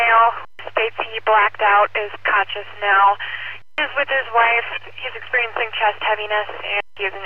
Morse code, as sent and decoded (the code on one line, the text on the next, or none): none